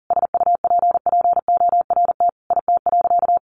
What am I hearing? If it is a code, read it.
Morse code, heard as a keyed tone, about 40 wpm, 700 Hz. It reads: SUPPORT IT.
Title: Sentences from Top 1300 Words 40wpm